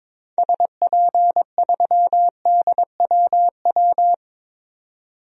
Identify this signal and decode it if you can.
Morse code, heard as a keyed tone, about 22 wpm, 700 Hz. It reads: SP3DWW
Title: DXCC-100 Call Signs - 6 Characters - 22wpm